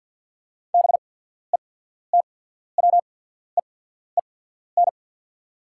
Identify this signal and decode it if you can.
Morse code, heard as a keyed tone, about 48 wpm, spaced as for 14 wpm, 690 Hz. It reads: BETWEEN
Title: 8x Character Spacing - Top 500 Words - 50wpm